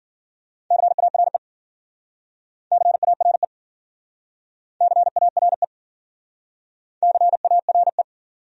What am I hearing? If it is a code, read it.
Morse code, heard as a keyed tone, about 54 wpm, 700 Hz. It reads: CARE CARE CARE CARE